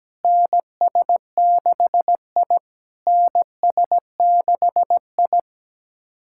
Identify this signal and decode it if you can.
Morse code, heard as a keyed tone, about 17 wpm, 700 Hz. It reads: NS6I NS6I
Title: SOTA QSO Practice 17wpm